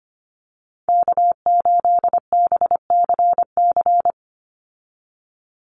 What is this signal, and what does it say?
Morse code, heard as a keyed tone, about 25 wpm, 690 Hz. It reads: K8BCC
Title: USA Call Signs 25wpm